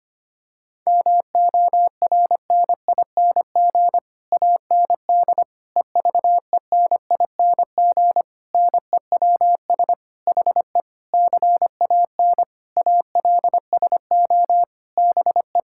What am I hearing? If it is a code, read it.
Morse code, heard as a keyed tone, about 25 wpm, 700 Hz. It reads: MORNING AND EVENING NEWS HE CAN ALSO BE
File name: Sets of 9 Words - 25wpm